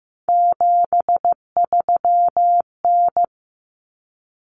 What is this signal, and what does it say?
Morse code, heard as a keyed tone, about 15 wpm, 700 Hz. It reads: 73N
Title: Common Cut Numbers - 3-Digit 15wpm